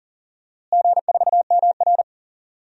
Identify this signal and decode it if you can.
Morse code, heard as a keyed tone, about 40 wpm, 700 Hz. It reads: G4MR
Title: DXCC-100 Call Signs - 4 Characters - Repeated 3x - 40wpm